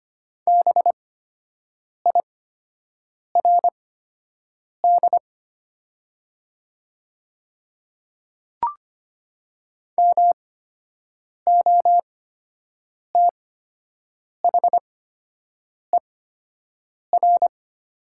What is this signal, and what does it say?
Morse code, heard as a keyed tone, about 25 wpm, spaced as for 7 wpm, 700 Hz. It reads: BIRD MOTHER
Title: 8x Character Spacing - Top 500 Words - 25wpm